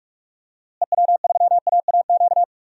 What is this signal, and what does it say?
Morse code, heard as a keyed tone, about 43 wpm, 700 Hz. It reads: EW3AAQ
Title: DXCC-100 Call Signs - 6 Characters - Repeated 2x - 45wpm